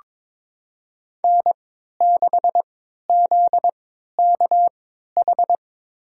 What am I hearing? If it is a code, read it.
Morse code, heard as a keyed tone, about 23 wpm, spaced as for 13 wpm, 700 Hz. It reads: N6ZKH
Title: USA Call Signs - 3x Character Spacing - 22wpm